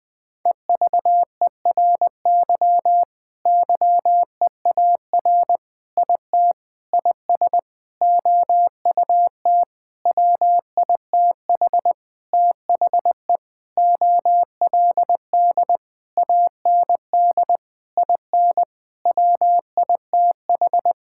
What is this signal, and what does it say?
Morse code, heard as a keyed tone, about 21 wpm, 700 Hz. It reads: EVERY YEAR IT IS OUT WITH THE OLD AND IN WITH
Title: Sentences with 9-Word Phrases - Spoken Beginning - 20wpm